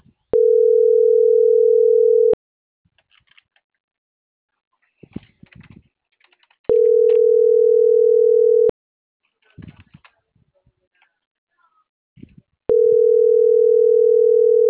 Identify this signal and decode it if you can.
Ringback tone (North America), 440 + 480 Hz.